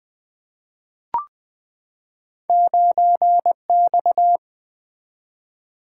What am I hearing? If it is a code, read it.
Morse code, heard as a keyed tone, about 20 wpm, 700 Hz. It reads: 9X
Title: Two Letter-Number Combinations - 20wpm